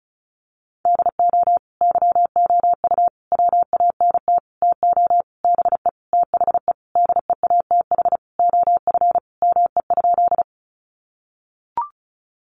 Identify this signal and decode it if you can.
Morse code, heard as a keyed tone, about 34 wpm, 700 Hz. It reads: DO YOU WANT TO BE THE DEATH OF ME?
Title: Sentences from Top 1100 Words 35wpm